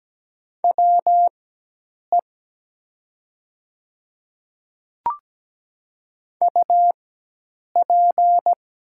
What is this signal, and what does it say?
Morse code, heard as a keyed tone, about 17 wpm, spaced as for 8 wpm, 700 Hz. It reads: WE UP
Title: 4x Character Spacing - Top 300 Words - 17wpm